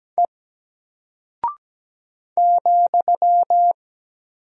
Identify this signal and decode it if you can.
Morse code, heard as a keyed tone, about 17 wpm, 700 Hz. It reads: E ,
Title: Single Character (Letter:Number:Punctuation:Prosign) Alt 17wpm